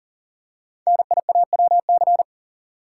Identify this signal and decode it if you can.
Morse code, heard as a keyed tone, about 40 wpm, 700 Hz. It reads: NIAWC
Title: Code Groups - 5 Letters - 40wpm